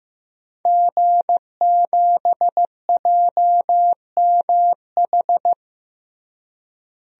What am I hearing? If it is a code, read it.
Morse code, heard as a keyed tone, about 15 wpm, 700 Hz. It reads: G7JMH
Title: DXCC-100 Call Signs - 5 Characters - Repeated 3x - 15wpm